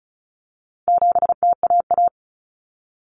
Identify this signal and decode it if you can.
Morse code, heard as a keyed tone, about 34 wpm, 700 Hz. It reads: 7TAA
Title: Common Cut Numbers - 4-Digit 35wpm